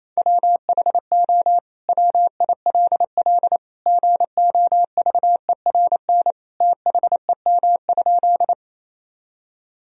Morse code, heard as a keyed tone, about 27 wpm, 700 Hz. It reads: WHO WILL GOVERN THEM?
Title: Sentences from Top 1000 Words - Review Part 2 - 28wpm